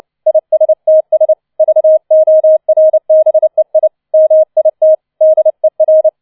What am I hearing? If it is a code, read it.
Morse code, heard as a keyed tone, about 29 wpm, 630 Hz. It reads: ISTS VORBEI MIT DER